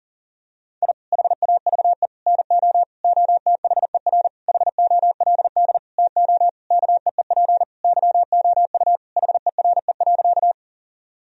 Morse code, heard as a keyed tone, about 40 wpm, 700 Hz. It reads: I HAVE NO OTHER HOLD TO KEEP YOU HERE.
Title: Sentences from Top 1000 Words - Review Part 2 - 40wpm